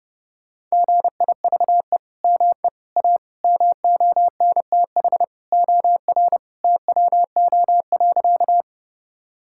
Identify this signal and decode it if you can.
Morse code, heard as a keyed tone, about 30 wpm, 700 Hz. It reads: GIVE ME A MONTH OR TWO.
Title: Sentences from Top 1000 Words - Review Part 5 - 30wpm